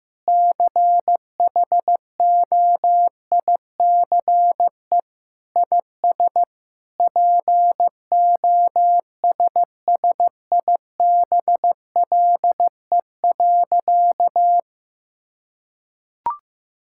Morse code, heard as a keyed tone, about 16 wpm, 700 Hz. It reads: CHOICE IS POSSIBLE.
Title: Sentences from Top 1000 Words - Review Part 6 - 15wpm